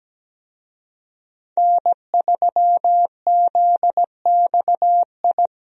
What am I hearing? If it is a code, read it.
Morse code, heard as a keyed tone, about 17 wpm, 700 Hz. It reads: N3ZXI
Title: USA Call Signs - 5 Characters - 17wpm